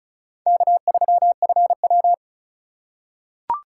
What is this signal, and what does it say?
Morse code, heard as a keyed tone, about 36 wpm, 700 Hz. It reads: K3FW
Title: USA Call Signs - 4 Characters - 3x Character Spacing - 35wpm